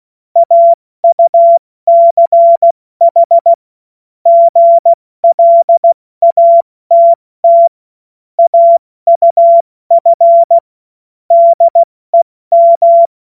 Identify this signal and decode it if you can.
Morse code, heard as a keyed tone, about 16 wpm, spaced as for 14 wpm, 680 Hz. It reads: AUCH GLATT AUF DEM
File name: Morsekurs / Witze / Witz 02 April 2024 70 F80